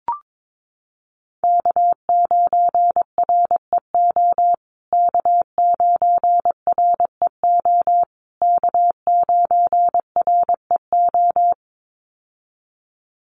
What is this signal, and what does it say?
Morse code, heard as a keyed tone, about 22 wpm, 700 Hz. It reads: K9REO K9REO K9REO
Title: USA Call Signs Repeated 3x 22wpm